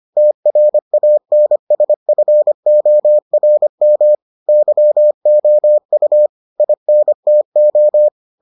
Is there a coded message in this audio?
Yes: Morse code from 0.2 to 8.1 s, heard as a keyed tone, about 25 wpm, 600 Hz. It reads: TRANSFORM YOU INTO